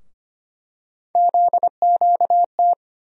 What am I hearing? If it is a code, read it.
Morse code, heard as a keyed tone, about 25 wpm, 700 Hz. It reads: ZQT
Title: USA Call Sign Postfixes - 3 Letters 25wpm